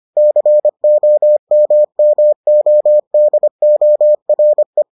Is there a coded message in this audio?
Yes: Morse code from 0.2 to 4.8 s, heard as a keyed tone, about 25 wpm, 600 Hz. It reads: COMMODORE